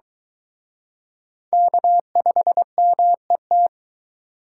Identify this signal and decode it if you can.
Morse code, heard as a keyed tone, about 22 wpm, 700 Hz. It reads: K5MET